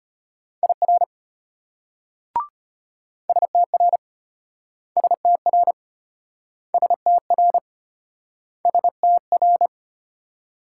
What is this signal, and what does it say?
Morse code, heard as a keyed tone, about 34 wpm, 700 Hz. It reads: IR STR STR STR STR